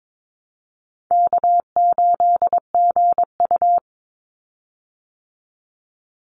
Morse code, heard as a keyed tone, about 22 wpm, 700 Hz. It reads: K8GU
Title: Easy Call Signs Repeated 3x - 22wpm